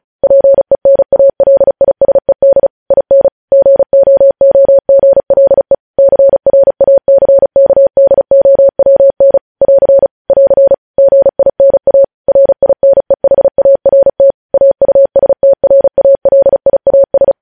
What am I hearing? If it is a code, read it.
Morse code, heard as a keyed tone, about 36 wpm, 550 Hz. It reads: PENALISED IN GOOGLE CRACKDOWN <AR> <AR> GINA RINEHART AUSTRALIAS